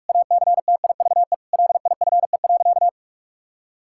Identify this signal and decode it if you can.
Morse code, heard as a keyed tone, about 47 wpm, 700 Hz. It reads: ACTIVE LIFE.
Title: Sentences from Top 1000 Words - Review Part 6 - 45wpm